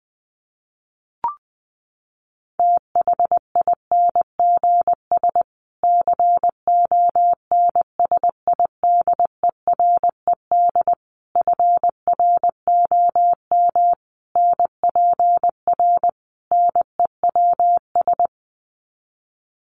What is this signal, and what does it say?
Morse code, heard as a keyed tone, about 20 wpm, 700 Hz. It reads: THINGS CONSIDERED FROM NPR NEWS
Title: Sets of 5 Words Encore 20wpm